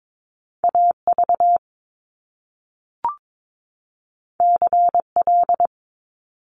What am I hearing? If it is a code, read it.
Morse code, heard as a keyed tone, about 22 wpm, 700 Hz. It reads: AV CL
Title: Top Two Letter Combinations in Words 22wpm